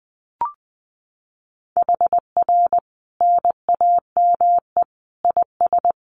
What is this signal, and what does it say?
Morse code, heard as a keyed tone, about 20 wpm, 700 Hz. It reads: HR NAME IS
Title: Common QSO Phrases 20wpm